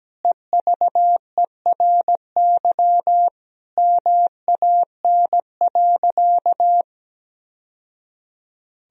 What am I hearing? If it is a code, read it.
Morse code, heard as a keyed tone, about 17 wpm, 700 Hz. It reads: EVERY MAN.
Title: Sentences from Top 1000 Words - Review Part 1 - 17wpm